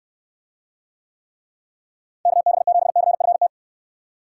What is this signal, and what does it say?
Morse code, heard as a keyed tone, about 69 wpm, 700 Hz. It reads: KB7CFT